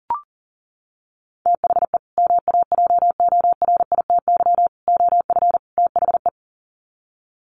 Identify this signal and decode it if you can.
Morse code, heard as a keyed tone, about 40 wpm, 700 Hz. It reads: THE MAJORITY OF THE
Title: Sets of 4 Words 40wpm